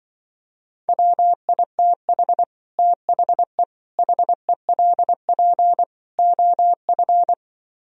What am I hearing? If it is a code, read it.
Morse code, heard as a keyed tone, about 24 wpm, 700 Hz. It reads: WITH THE HELP OF
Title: Speed-Racing - Sets of 4 Words 20wpm